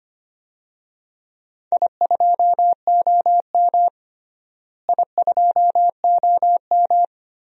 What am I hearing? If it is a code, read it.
Morse code, heard as a keyed tone, about 25 wpm, 700 Hz. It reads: I2OM I2OM
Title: DXCC-100 Call Signs - 4 Characters - Repeated 3x - 25wpm